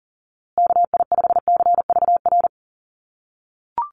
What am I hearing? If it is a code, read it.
Morse code, heard as a keyed tone, about 40 wpm, 700 Hz. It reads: KI5CVR
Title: USA Call Signs Repeated 2x 40wpm